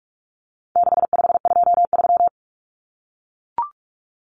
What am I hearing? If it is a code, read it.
Morse code, heard as a keyed tone, about 44 wpm, 700 Hz. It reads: 6523
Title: Numbers - 4-Digit 45wpm